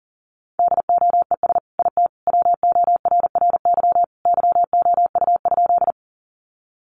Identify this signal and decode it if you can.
Morse code, heard as a keyed tone, about 40 wpm, 700 Hz. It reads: DOES IT WORRY YOU?
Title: Sentences from Top 1100 Words 40wpm